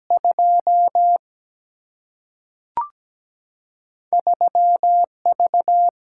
Morse code, heard as a keyed tone, about 17 wpm, 700 Hz. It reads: 2 3V
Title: Single-Letter Single-Number 17wpm